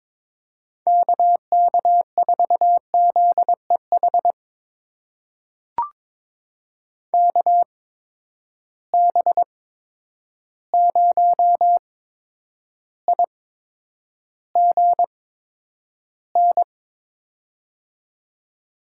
Morse code, heard as a keyed tone, about 22 wpm, spaced as for 9 wpm, 700 Hz. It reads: KK4ZEH KB0IGN
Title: USA Call Signs - 6 Characters - 8x Character Spacing - 22wpm